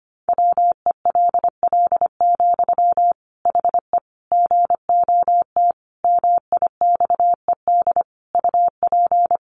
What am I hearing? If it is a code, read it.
Morse code, heard as a keyed tone, about 25 wpm, 700 Hz. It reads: WELL, HE GOT MIXED UP
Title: Sentences from Top 1500 Words 25wpm